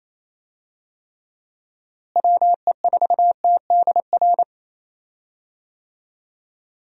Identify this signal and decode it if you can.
Morse code, heard as a keyed tone, about 28 wpm, 700 Hz. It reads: WE4TDR
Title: USA Call Signs - 6 Characters - Repeated 3x - 28wpm